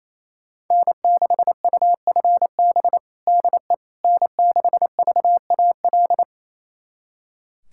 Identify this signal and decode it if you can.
Morse code, heard as a keyed tone, about 28 wpm, 700 Hz. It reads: N6UFB DE N6VAL
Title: USA Callsign DE Callsign - 5 Characters - 28wpm